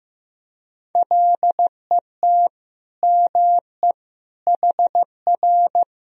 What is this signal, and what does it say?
Morse code, heard as a keyed tone, about 16 wpm, 700 Hz. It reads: LET ME HR
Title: Sentences from Top 400 Words and Abbreviations 15wpm